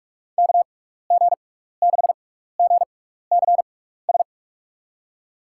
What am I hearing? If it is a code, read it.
Morse code, heard as a keyed tone, about 46 wpm, spaced as for 15 wpm, 700 Hz. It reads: KG6GCS
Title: USA Call Signs - 6 Characters - 6x Character Spacing - 45wpm